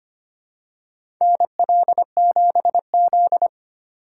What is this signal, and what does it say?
Morse code, heard as a keyed tone, about 25 wpm, 700 Hz. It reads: NL7Z
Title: USA Call Signs - 4 Characters - 3x Character Spacing - 25wpm